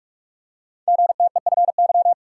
Morse code, heard as a keyed tone, about 44 wpm, 700 Hz. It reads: GTEFY
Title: Code Groups - 5 Letter:Numbers - 45wpm